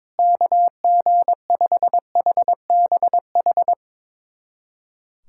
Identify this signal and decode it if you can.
Morse code, heard as a keyed tone, about 22 wpm, 700 Hz. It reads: KG5HBH
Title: USA Callsign DE Callsign - 6 Characters - 22wpm